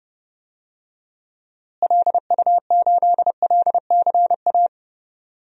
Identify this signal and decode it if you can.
Morse code, heard as a keyed tone, about 30 wpm, 700 Hz. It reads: LU8LCA
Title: DXCC-100 Call Signs - 6 Characters - Repeated 2x - 30wpm